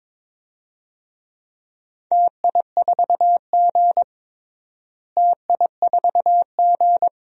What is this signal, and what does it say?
Morse code, heard as a keyed tone, about 22 wpm, 700 Hz. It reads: TI4G TI4G
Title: DXCC-100 Call Signs - 4 Characters - Repeated 2x - 22wpm